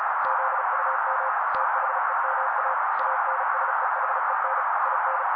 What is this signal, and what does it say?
Morse code, heard as a keyed tone, about 35 wpm, 530 Hz. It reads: ZUM THEMA ANSEHEN IN
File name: daily / 35wpm noise